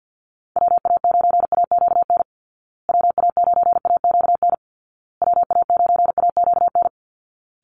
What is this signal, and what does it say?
Morse code, heard as a keyed tone, about 50 wpm, 700 Hz. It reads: WA9AQN WA9AQN WA9AQN
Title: Call Signs Repeated 2x - 50wpm